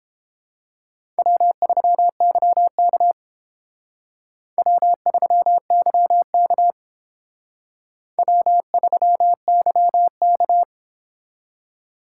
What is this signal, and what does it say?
Morse code, heard as a keyed tone, about 29 wpm, 700 Hz. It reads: W3YK W3YK W3YK